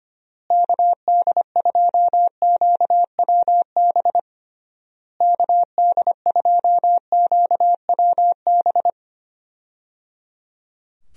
Morse code, heard as a keyed tone, about 25 wpm, 700 Hz. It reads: KD2QWB KD2QWB
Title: USA Call Signs - 6 Characters - Repeated 3x - 25wpm